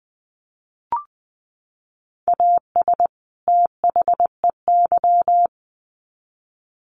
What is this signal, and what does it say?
Morse code, heard as a keyed tone, about 20 wpm, 700 Hz. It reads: AS THEY
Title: Sets of 2 Words 20wpm